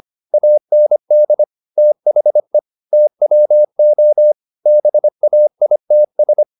Morse code, heard as a keyed tone, about 25 wpm, 600 Hz. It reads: AND THE TWO BAITS
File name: the old man and the sea part 1 25wpm